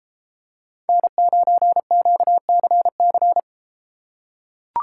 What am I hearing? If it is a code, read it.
Morse code, heard as a keyed tone, about 33 wpm, 700 Hz. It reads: N9QCC